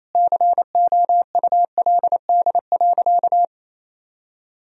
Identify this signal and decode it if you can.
Morse code, heard as a keyed tone, about 27 wpm, 700 Hz. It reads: COULD.
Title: Sentences from Top 1000 Words - Review Part 3 - 28wpm